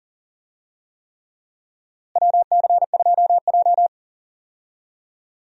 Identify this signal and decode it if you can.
Morse code, heard as a keyed tone, about 40 wpm, 700 Hz. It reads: WC2J
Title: USA Call Signs - 4 Characters - Repeated 3x - 40wpm